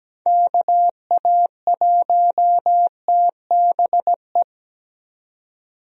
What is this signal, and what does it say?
Morse code, heard as a keyed tone, about 17 wpm, 700 Hz. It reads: KA1TBE
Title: USA Call Signs - 6 Characters - 17wpm